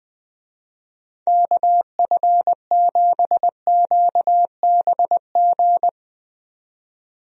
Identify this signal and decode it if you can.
Morse code, heard as a keyed tone, about 21 wpm, 700 Hz. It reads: KF7QBG
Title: USA Call Signs - 6 Characters - 20wpm